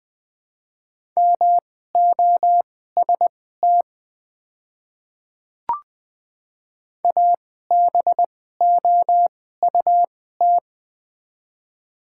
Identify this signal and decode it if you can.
Morse code, heard as a keyed tone, about 20 wpm, spaced as for 14 wpm, 700 Hz. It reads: MOST ABOUT
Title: 2x Character Spacing - Top 100 Words - 20wpm